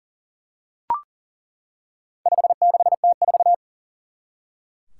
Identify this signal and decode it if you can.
Morse code, heard as a keyed tone, about 42 wpm, 700 Hz. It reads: 56T4
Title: Four Challenging Letter-Number Combinations 40wpm